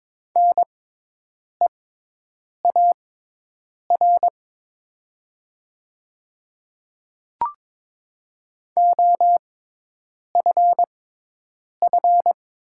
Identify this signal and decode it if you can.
Morse code, heard as a keyed tone, about 22 wpm, spaced as for 8 wpm, 700 Hz. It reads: NEAR OFF
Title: 6x Character Spacing - Top 500 Words - 22wpm